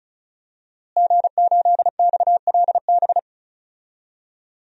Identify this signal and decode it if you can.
Morse code, heard as a keyed tone, about 34 wpm, 700 Hz. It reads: G8XLB
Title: DXCC-100 Call Signs - 5 Characters - Repeated 3x - 35wpm